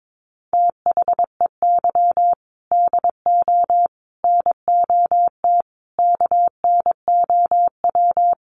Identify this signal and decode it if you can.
Morse code, heard as a keyed tone, about 22 wpm, 700 Hz. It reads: THEY DO NOT KNOW
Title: Sets of 5 Words Encore 22wpm